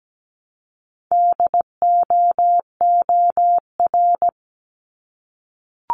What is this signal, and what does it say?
Morse code, heard as a keyed tone, about 17 wpm, 700 Hz. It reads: DOOR
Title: Multi-letter Phonemes Repeated 3x 17wpm